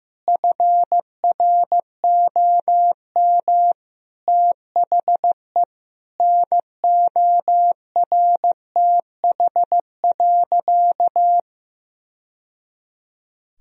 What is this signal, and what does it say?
Morse code, heard as a keyed tone, about 16 wpm, 700 Hz. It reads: FROM THE NORTH.
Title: Sentences from Top 1000 Words - Review Part 2 - 15wpm